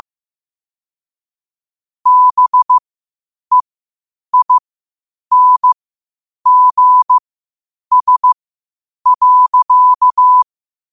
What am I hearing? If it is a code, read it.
Morse code, heard as a keyed tone, about 15 wpm, spaced as for 8 wpm, 1000 Hz. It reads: BEINGS.
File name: qotdfiles / 8wpm